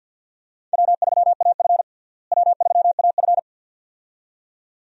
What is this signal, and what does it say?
Morse code, heard as a keyed tone, about 47 wpm, 700 Hz. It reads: W3AF W3AF
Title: USA Call Signs - 4 Characters - Repeated 2x - 50wpm